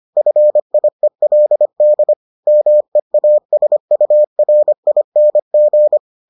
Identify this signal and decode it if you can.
Morse code, heard as a keyed tone, about 25 wpm, 600 Hz. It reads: FIELD MEASURING